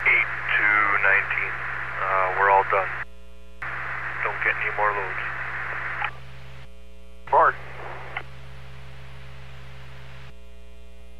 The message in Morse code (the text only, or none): none